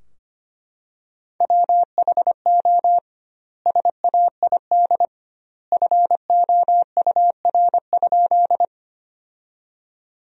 Morse code, heard as a keyed tone, about 25 wpm, 700 Hz. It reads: WHO SAID FOUR?
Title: 2x Word-Spacing - Sentences from Top 300 Words - 25wpm